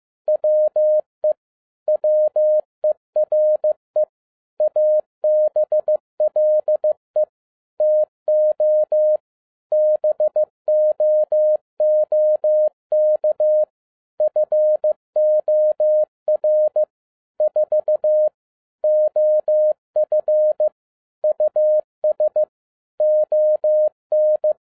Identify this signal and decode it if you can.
Morse code, heard as a keyed tone, about 15 wpm, 610 Hz. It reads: WE WERE ABLE TO BOOK FOR 4 OF US ON
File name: 2011-07-mp3 / n12 T8XO 15wpm